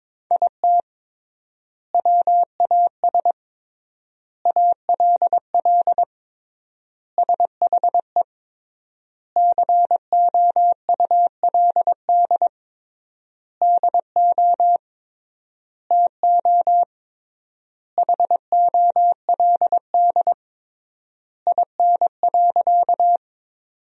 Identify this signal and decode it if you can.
Morse code, heard as a keyed tone, about 22 wpm, 700 Hz. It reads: IT WAS ALL SHE COULD DO TO HOLD IN.
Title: 3x Word-Spacing - Sentences from Top 400 Words - 22wpm